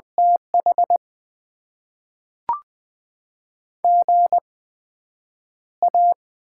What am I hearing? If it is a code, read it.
Morse code, heard as a keyed tone, about 20 wpm, spaced as for 5 wpm, 700 Hz. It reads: THGA
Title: Top 20 QSO Elements - 8x Character Spacing - 20wpm